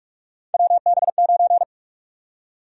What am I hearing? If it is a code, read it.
Morse code, heard as a keyed tone, about 46 wpm, 700 Hz. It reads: WB9
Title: USA Call Sign Prefixes - Repeated 2x - 45wpm